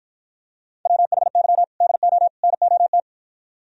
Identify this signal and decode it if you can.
Morse code, heard as a keyed tone, about 52 wpm, 700 Hz. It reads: WHY DO NOT